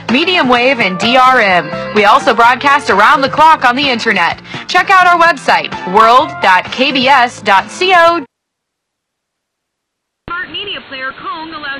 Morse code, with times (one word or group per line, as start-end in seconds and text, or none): none